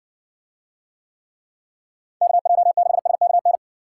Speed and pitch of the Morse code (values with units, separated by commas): 59 wpm, 700 Hz